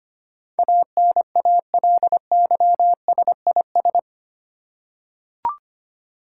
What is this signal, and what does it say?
Morse code, heard as a keyed tone, about 25 wpm, 700 Hz. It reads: ANALYSIS
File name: Eight Letter Words - 25wpm